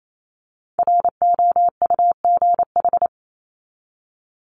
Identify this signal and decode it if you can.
Morse code, heard as a keyed tone, about 28 wpm, 700 Hz. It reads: ROUGH
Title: Top 1501-1750 Words 28wpm